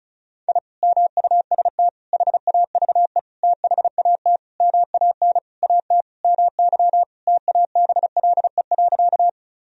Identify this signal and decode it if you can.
Morse code, heard as a keyed tone, about 36 wpm, 700 Hz. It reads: I MUST HAVE THAT MAN AT MY TABLE.
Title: Sentences from Top 1000 Words - Review Part 2 - 35wpm